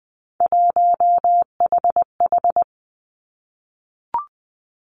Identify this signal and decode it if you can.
Morse code, heard as a keyed tone, about 20 wpm, 700 Hz. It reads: 1HH
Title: Three Challenging Letter-Number Combinations 20wpm